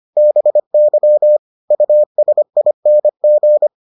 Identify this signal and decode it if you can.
Morse code, heard as a keyed tone, about 25 wpm, 600 Hz. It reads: BY USING